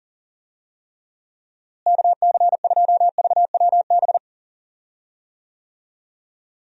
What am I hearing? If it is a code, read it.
Morse code, heard as a keyed tone, about 40 wpm, 700 Hz. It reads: KC2VWB
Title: USA Call Signs - 6 Characters - Repeated 3x - 40wpm